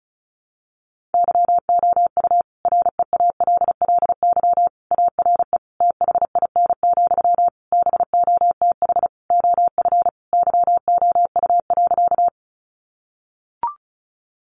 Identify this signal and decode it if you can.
Morse code, heard as a keyed tone, about 34 wpm, 700 Hz. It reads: YOU REALLY ARE THIN, BOTH OF YOU.
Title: Sentences from Top 1200 Words 35wpm